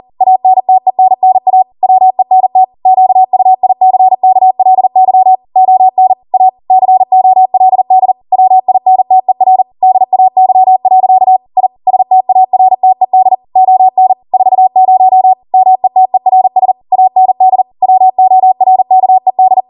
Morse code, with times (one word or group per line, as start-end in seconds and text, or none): 0.2-1.6 ANTENNA
1.8-2.6 WENT
2.9-5.3 QUICKLY
5.6-6.1 ON
6.3-6.5 A
6.7-8.1 COLD
8.3-9.6 WINTER
9.8-11.4 DAY.
11.6-11.7 I
11.9-13.3 STARTED
13.6-14.1 ON
14.3-15.3 40
15.5-16.7 METERS
16.9-17.6 AND
17.8-19.6 WORKED